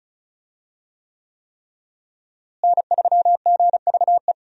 Morse code, heard as a keyed tone, about 34 wpm, 700 Hz. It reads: N3GVE
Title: USA Call Signs - 5 Characters - Repeated 3x - 35wpm